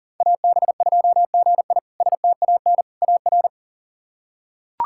Morse code, heard as a keyed tone, about 40 wpm, 700 Hz. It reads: AB2GI STAN AR
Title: K1USN SST QSO Practice - 40wpm